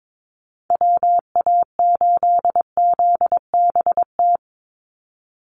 Morse code, heard as a keyed tone, about 22 wpm, 700 Hz. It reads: WA8ZBT
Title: Call Signs Repeated 3x - 22wpm